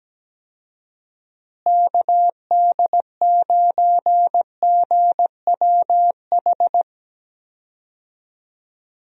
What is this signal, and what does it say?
Morse code, heard as a keyed tone, about 17 wpm, 700 Hz. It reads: KD9GWH
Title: USA Call Signs - 6 Characters - Repeated 2x - 17wpm